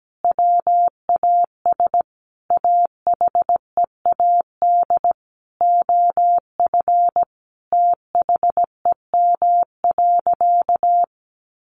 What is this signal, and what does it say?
Morse code, heard as a keyed tone, about 17 wpm, 700 Hz. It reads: WAS AHEAD OF THEM.
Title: Sentences from Top 1100 Words 17wpm